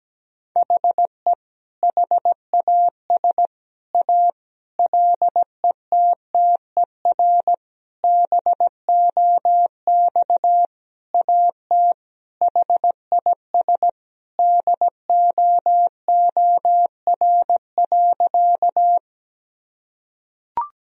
Morse code, heard as a keyed tone, about 17 wpm, 700 Hz. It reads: HE HAS A LETTER BOX AT HIS DOOR.
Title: Sentences from Top 1000 Words - Review Part 2 - 17wpm